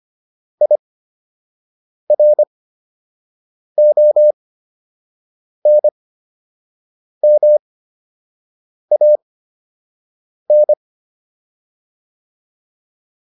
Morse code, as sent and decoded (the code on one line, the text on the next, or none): .. .-. --- -. -- .- -.
IRONMAN